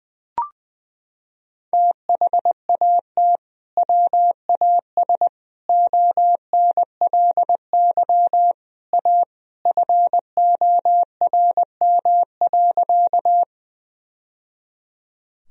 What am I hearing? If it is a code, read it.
Morse code, heard as a keyed tone, about 20 wpm, 700 Hz. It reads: THAT WAS ONLY A FORM.
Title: Sentences from Top 1000 Words - Review Part 1 - 20wpm